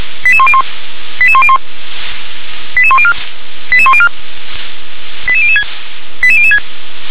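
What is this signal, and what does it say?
Five-tone selective call (ZVEI), heard as quick runs of five tones, tones of about 70 ms. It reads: 88101 88101 88104 88104 88006 88006
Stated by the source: ZVEI five-tone selective call received on SDR, code 88742.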